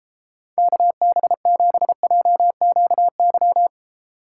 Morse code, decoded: KB7JQY

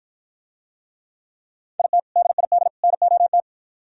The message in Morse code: IT DID NOT

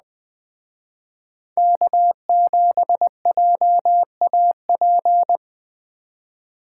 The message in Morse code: K7JAP

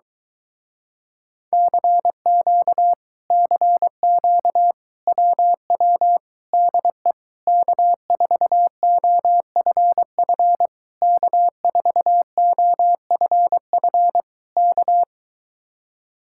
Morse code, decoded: CQ CQ WW DE K4OFF K4OFF K

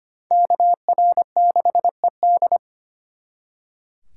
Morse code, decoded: KR6ED

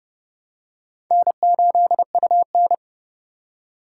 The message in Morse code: N8UN

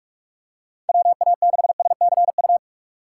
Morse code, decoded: WA6SCU

